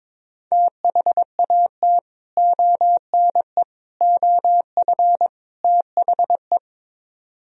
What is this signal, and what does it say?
Morse code, heard as a keyed tone, about 22 wpm, 700 Hz. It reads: THAT ONE OF THE
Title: Speed-Racing - Sets of 4 Words 22wpm